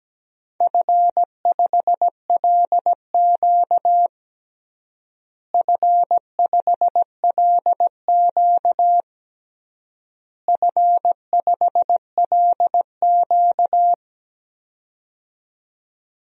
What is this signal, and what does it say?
Morse code, heard as a keyed tone, about 17 wpm, 700 Hz. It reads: F5LQ F5LQ F5LQ
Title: DXCC-100 Call Signs - 4 Characters - Repeated 3x - 17wpm